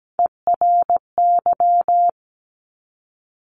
Morse code, heard as a keyed tone, about 17 wpm, 700 Hz. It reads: ERY
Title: Top Three Letter Combinations in Words 17wpm